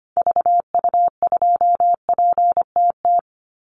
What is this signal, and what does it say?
Morse code, heard as a keyed tone, about 25 wpm, 700 Hz. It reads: VU2PTT
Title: Call Signs Repeated 3x - 25wpm